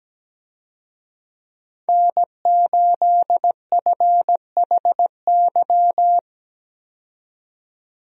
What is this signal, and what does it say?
Morse code, heard as a keyed tone, about 17 wpm, 700 Hz. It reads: N8FHY
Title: USA Call Signs - 5 Characters - 17wpm